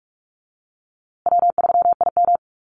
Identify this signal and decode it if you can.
Morse code, heard as a keyed tone, about 46 wpm, 700 Hz. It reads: W3IM
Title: USA Call Signs Repeated 2x 45wpm